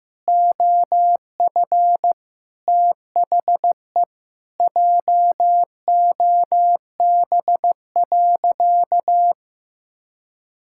Morse code, heard as a keyed tone, about 15 wpm, 700 Hz. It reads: OF THE JOB.
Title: Sentences from Top 1000 Words - Review Part 5 - 15wpm